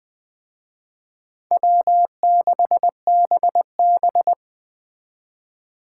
Morse code, heard as a keyed tone, about 20 wpm, 700 Hz. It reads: W6BB
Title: Speed Racing - USA 1x2 Call Signs 17wpm